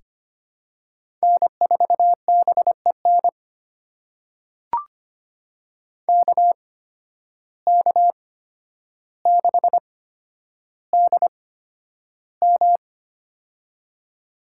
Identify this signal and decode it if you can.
Morse code, heard as a keyed tone, about 25 wpm, spaced as for 11 wpm, 700 Hz. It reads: N4BEN KK6DM
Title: USA Call Signs - 5 Characters - 8x Character Spacing - 25wpm